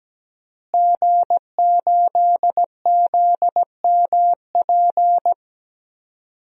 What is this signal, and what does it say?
Morse code, heard as a keyed tone, about 17 wpm, 700 Hz. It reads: G8ZMP
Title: DXCC-100 Call Signs - 5 Characters - Repeated 3x - 17wpm